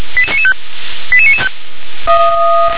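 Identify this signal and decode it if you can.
Five-tone selective call (ZVEI), heard as quick runs of five tones, tones of about 70 ms. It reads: 88995
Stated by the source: ZVEI five-tone selective call received on SDR.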